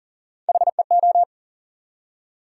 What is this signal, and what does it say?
Morse code, heard as a keyed tone, about 40 wpm, 700 Hz. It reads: HEO
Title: USA Call Sign Postfixes - 3 Letters 40wpm